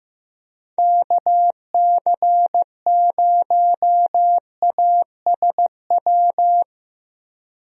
Morse code, heard as a keyed tone, about 15 wpm, 700 Hz. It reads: KC0ASW
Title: USA Call Signs - 6 Characters - 15wpm